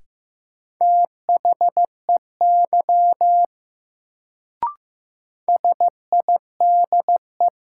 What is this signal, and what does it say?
Morse code, heard as a keyed tone, about 15 wpm, 700 Hz. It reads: THEY SIDE
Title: Sets of 2 Words - New Word Drill - 15wpm